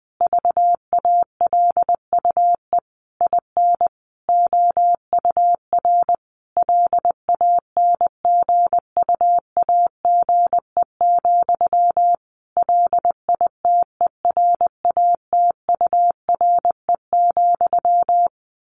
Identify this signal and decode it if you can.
Morse code, heard as a keyed tone, about 20 wpm, 700 Hz. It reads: VALUE IN OUR LANGUAGE, LITERATURE,
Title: Sentences from Top 600 Words 20wpm